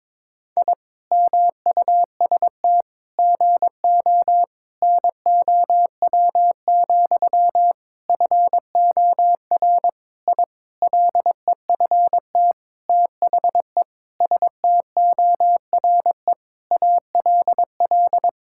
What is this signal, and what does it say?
Morse code, heard as a keyed tone, about 22 wpm, 700 Hz. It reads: I MUST GO NOW, FOR I LEFT THE STORE ALL
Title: Sentences from Top 1000 Words - Review Part 5 - 22wpm